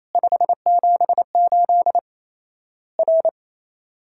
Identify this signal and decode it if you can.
Morse code, heard as a keyed tone, about 28 wpm, 700 Hz. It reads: 578 R
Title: Worked All Europe DX Contest - QTC Exchange - 28wpm